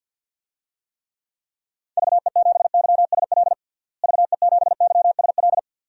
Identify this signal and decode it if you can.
Morse code, heard as a keyed tone, about 47 wpm, 700 Hz. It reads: VE7YSL VE7YSL
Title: DXCC-100 Call Signs - 6 Characters - Repeated 2x - 50wpm